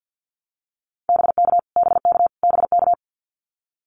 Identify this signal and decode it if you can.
Morse code, heard as a keyed tone, about 50 wpm, 700 Hz. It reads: BK BK BK
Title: Abbreviations Repeated 3x - 50wpm